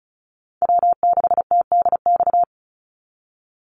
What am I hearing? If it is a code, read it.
Morse code, heard as a keyed tone, about 34 wpm, 700 Hz. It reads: W6TDX
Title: NA SOTA Chaser Callsigns 35wpm